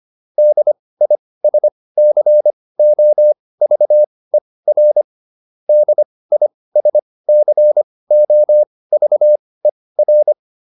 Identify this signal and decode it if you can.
Morse code, heard as a keyed tone, about 25 wpm, spaced as for 18 wpm, 600 Hz. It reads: DISCOVER DISCOVER
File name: DIS8 18